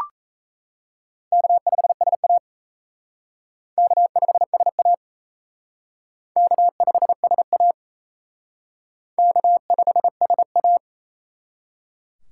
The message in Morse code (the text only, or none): K5SA K5SA K5SA K5SA